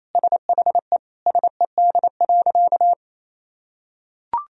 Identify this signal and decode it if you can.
Morse code, heard as a keyed tone, about 27 wpm, 700 Hz. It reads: SHE SED.
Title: Sentences from Top 100 Words and Abbreviations 28wpm